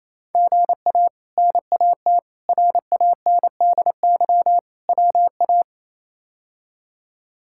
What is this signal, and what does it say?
Morse code, heard as a keyed tone, about 28 wpm, 700 Hz. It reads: GA NAT RANDY WA
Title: K1USN SST QSO Practice - 28wpm